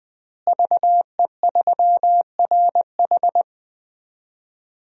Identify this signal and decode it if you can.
Morse code, heard as a keyed tone, about 20 wpm, 700 Hz. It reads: VE3RH
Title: DXCC-100 Call Signs - 5 Characters - Repeated 2x - 20wpm